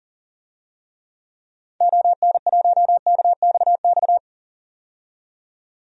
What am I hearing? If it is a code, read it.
Morse code, heard as a keyed tone, about 40 wpm, 700 Hz. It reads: ON1KXX
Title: DXCC-100 Call Signs - 6 Characters - 40wpm